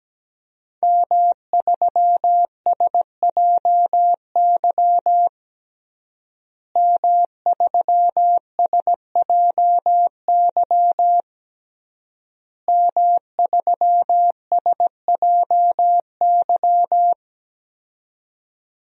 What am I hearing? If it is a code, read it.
Morse code, heard as a keyed tone, about 17 wpm, 700 Hz. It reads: M3SJY M3SJY M3SJY
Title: DXCC-100 Call Signs - 5 Characters - Repeated 3x - 17wpm